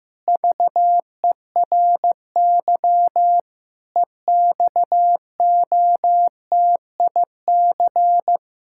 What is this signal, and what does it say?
Morse code, heard as a keyed tone, about 16 wpm, 700 Hz. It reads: VERY EXOTIC
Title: Pangram Challenge v2 - 15wpm